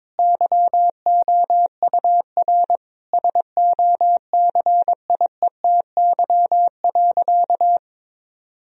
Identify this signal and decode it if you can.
Morse code, heard as a keyed tone, about 22 wpm, 700 Hz. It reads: YOUR SOCIETY.
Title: Sentences from Top 1000 Words - Review Part 4 - 22wpm